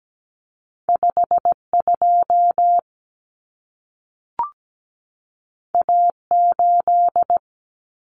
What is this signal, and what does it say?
Morse code, heard as a keyed tone, about 17 wpm, 700 Hz. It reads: 52 A8
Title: Common Cut Numbers - 2-Digit 17wpm